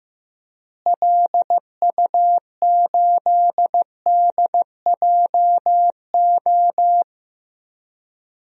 Morse code, heard as a keyed tone, about 15 wpm, 700 Hz. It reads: LU8DJO
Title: DXCC-100 Call Signs - 6 Characters - Repeated 3x - 15wpm